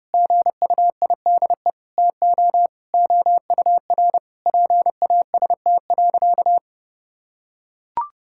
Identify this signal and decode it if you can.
Morse code, heard as a keyed tone, about 30 wpm, 700 Hz. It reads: GUIDE TO OUR PAST.
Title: Sentences from Top 1000 Words - Review Part 6 - 30wpm